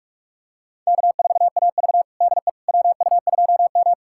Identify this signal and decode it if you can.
Morse code, heard as a keyed tone, about 41 wpm, 700 Hz. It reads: K4AV DE WU2M
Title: USA Callsign DE Callsign - 4 Characters - 45wpm